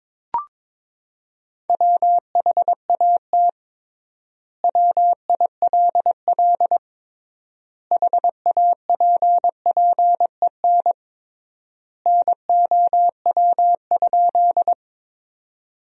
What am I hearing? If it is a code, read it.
Morse code, heard as a keyed tone, about 22 wpm, 700 Hz. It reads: WHAT WILL HAPPEN NOW?
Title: 3x Word-Spacing - Sentences from Top 400 Words - 22wpm